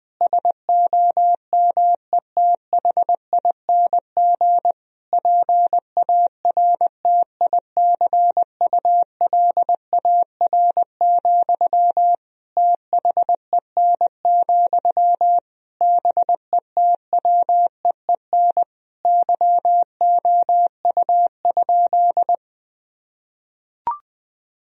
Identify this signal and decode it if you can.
Morse code, heard as a keyed tone, about 20 wpm, 700 Hz. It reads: SOMETHING PARTICULAR, THEN, BETWEEN YOU?
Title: Sentences from Top 1000 Words - Review Part 4 - 20wpm